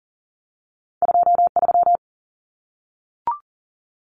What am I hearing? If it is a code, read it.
Morse code, heard as a keyed tone, about 40 wpm, 700 Hz. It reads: 23
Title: Common Cut Numbers - 2-Digit 40wpm